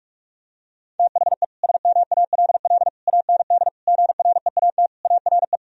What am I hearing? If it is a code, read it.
Morse code, heard as a keyed tone, about 41 wpm, 700 Hz. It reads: THE SMALL AND GREAT ARE
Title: Sentences from Top 1000 Words - Review Part 1 - 45wpm